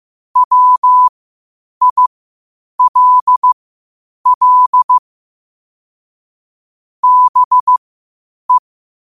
Morse code, heard as a keyed tone, about 15 wpm, spaced as for 8 wpm, 1000 Hz. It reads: WILL BE
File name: qotdfiles / 8wpm